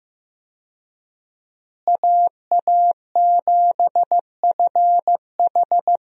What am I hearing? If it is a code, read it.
Morse code, heard as a keyed tone, about 15 wpm, 700 Hz. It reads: AA7FH
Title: USA Call Signs - 5 Characters - Repeated 3x - 15wpm